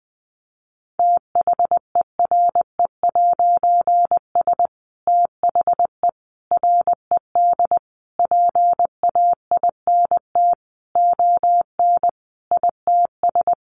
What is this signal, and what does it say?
Morse code, heard as a keyed tone, about 20 wpm, 700 Hz. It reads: THERE'S THE RED PAINT ON ITS
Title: A Sound Of Thunder 20wpm